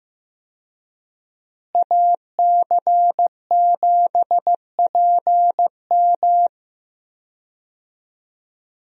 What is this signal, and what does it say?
Morse code, heard as a keyed tone, about 16 wpm, 700 Hz. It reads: AC7PM
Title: USA Call Signs - 5 Characters - Repeated 2x - 15wpm